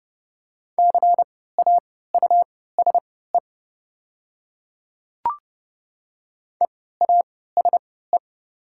Morse code, heard as a keyed tone, about 30 wpm, spaced as for 17 wpm, 700 Hz. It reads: CAUSE EASE
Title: 3x Character Spacing - Top 300 Words - 30wpm